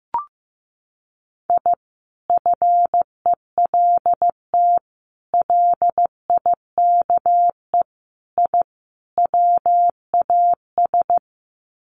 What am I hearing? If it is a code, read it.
Morse code, heard as a keyed tone, about 15 wpm, 700 Hz. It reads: I FELT LIKE I WAS
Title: Sets of 5 Words Encore 15wpm